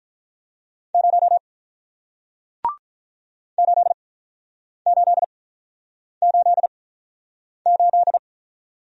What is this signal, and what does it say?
Morse code, heard as a keyed tone, about 47 wpm, spaced as for 9 wpm, 700 Hz. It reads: 0 8888